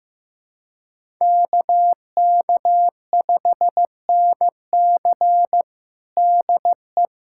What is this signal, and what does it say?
Morse code, heard as a keyed tone, about 16 wpm, 700 Hz. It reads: KK5NC DE
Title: USA Callsign DE Callsign 15wpm